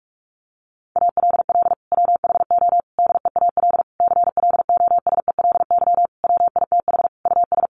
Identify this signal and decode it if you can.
Morse code, heard as a keyed tone, about 44 wpm, 700 Hz. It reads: ALL WHO DEAL CLOSELY WITH US
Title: Sentences from Top 1400 Words 45wpm